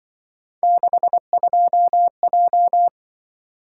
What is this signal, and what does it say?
Morse code, heard as a keyed tone, about 25 wpm, 700 Hz. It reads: 62J
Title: Speed-Racing - Three Letter-Number Combinations - 20wpm